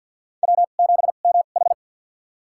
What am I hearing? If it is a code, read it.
Morse code, heard as a keyed tone, about 53 wpm, spaced as for 37 wpm, 700 Hz. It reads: W7MH
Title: USA Call Signs - 4 Characters - 2x Character Spacing - 50wpm